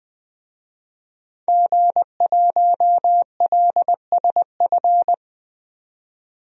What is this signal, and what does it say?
Morse code, heard as a keyed tone, about 20 wpm, 700 Hz. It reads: G1LSF
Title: DXCC-100 Call Signs - 5 Characters - 20wpm